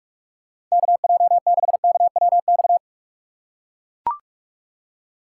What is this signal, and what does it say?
Morse code, heard as a keyed tone, about 44 wpm, 700 Hz. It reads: KJ6KWX
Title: USA Call Signs - 6 Characters - 6x Character Spacing - 45wpm